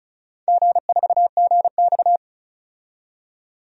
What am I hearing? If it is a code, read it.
Morse code, heard as a keyed tone, about 36 wpm, 700 Hz. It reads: G4GX
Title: DXCC-100 Call Signs - 4 Characters - 35wpm